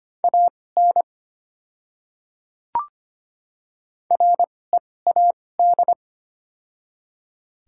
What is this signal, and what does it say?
Morse code, heard as a keyed tone, about 25 wpm, spaced as for 18 wpm, 700 Hz. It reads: AN READ
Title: 2x Character Spacing - Top 500 Words - 25wpm